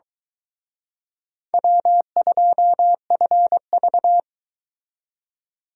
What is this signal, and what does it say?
Morse code, heard as a keyed tone, about 22 wpm, 700 Hz. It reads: W2FV